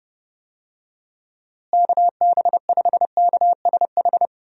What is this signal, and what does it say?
Morse code, heard as a keyed tone, about 32 wpm, 700 Hz. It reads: KB5KSH
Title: USA Call Signs - 6 Characters - Repeated 2x - 30wpm